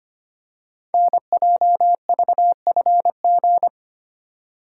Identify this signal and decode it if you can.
Morse code, heard as a keyed tone, about 25 wpm, 700 Hz. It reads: NJVFG
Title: Code Groups - 5 Letters - 25wpm